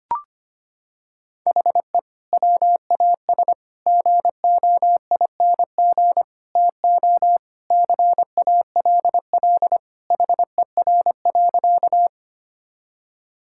Morse code, heard as a keyed tone, about 25 wpm, 700 Hz. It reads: HE WAS GOING TO CALL HER.
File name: practice / 25wpm / Sentences with 4-Word Phrases - 25wpm